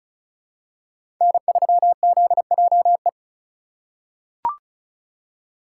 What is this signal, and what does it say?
Morse code, heard as a keyed tone, about 36 wpm, 700 Hz. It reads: N3ZJE
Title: USA Call Signs - 5 Characters - 3x Character Spacing - 35wpm